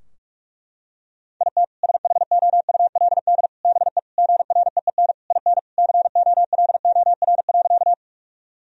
Morse code, heard as a keyed tone, about 46 wpm, 700 Hz. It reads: IT SHOULD BE GREEN IN COLOR.
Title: Sentences from Top 1000 Words - Review Part 2 - 45wpm